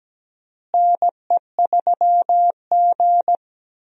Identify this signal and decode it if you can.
Morse code, heard as a keyed tone, about 17 wpm, 700 Hz. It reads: NE3G
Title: USA Call Signs - 4 Characters - Repeated 3x - 17wpm